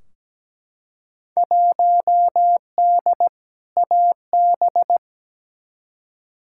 Morse code, heard as a keyed tone, about 17 wpm, 700 Hz. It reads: 1D AB
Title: Field Day Exchanges Encore - 17wpm